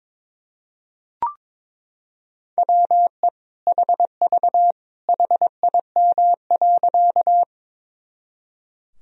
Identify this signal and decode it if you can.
Morse code, heard as a keyed tone, about 21 wpm, 700 Hz. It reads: WE HV HIM.
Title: Sentences from Top 100 Words and Abbreviations 22wpm